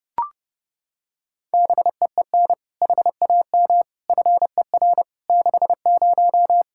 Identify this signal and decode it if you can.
Morse code, heard as a keyed tone, about 30 wpm, 700 Hz. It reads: BEEN HAM FER 60
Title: K1USN SST QSO Practice 30wpm